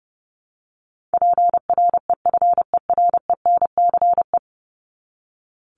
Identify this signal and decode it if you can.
Morse code, heard as a keyed tone, about 30 wpm, 700 Hz. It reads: PREFERENCE
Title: Top 1251-1500 Words 30wpm